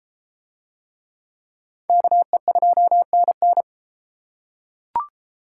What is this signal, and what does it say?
Morse code, heard as a keyed tone, about 30 wpm, 700 Hz. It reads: KE2NN